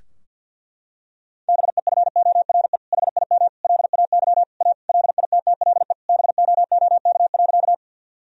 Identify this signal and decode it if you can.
Morse code, heard as a keyed tone, about 51 wpm, 700 Hz. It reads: BEFORE HIM LAY A LITTLE BOOK.
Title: Sentences from Top 1000 Words - Review Part 2 - 50wpm